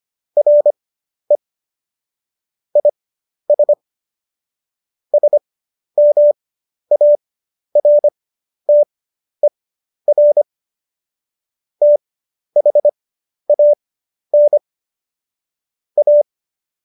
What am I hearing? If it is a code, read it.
Morse code, heard as a keyed tone, about 25 wpm, spaced as for 11 wpm, 590 Hz. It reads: RE IS SMARTER THAN A
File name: Jokes-25-char-10-eff